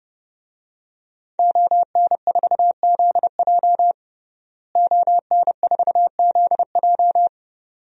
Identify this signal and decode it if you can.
Morse code, heard as a keyed tone, about 30 wpm, 700 Hz. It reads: ON4ZJ ON4ZJ
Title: DXCC-100 Call Signs - 5 Characters - Repeated 2x - 30wpm